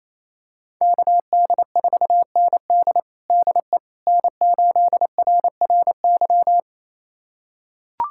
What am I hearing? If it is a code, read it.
Morse code, heard as a keyed tone, about 27 wpm, 700 Hz. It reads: KD4ND DE N8RRY
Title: USA Callsign DE Callsign - 5 Characters - 28wpm